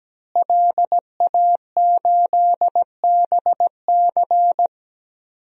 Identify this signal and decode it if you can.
Morse code, heard as a keyed tone, about 17 wpm, 700 Hz. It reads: LA8BC
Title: DXCC-100 Call Signs - 5 Characters - Repeated 2x - 17wpm